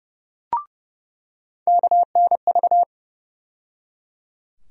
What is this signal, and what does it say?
Morse code, heard as a keyed tone, about 32 wpm, 700 Hz. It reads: KNV
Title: USA Call Sign Postfixes - 3 Letters 30wpm